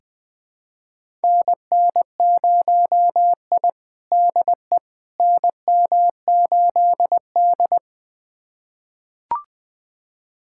Morse code, heard as a keyed tone, about 21 wpm, 700 Hz. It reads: NN0I DE NM8D
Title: USA Callsign DE Callsign - 4 Characters - 20wpm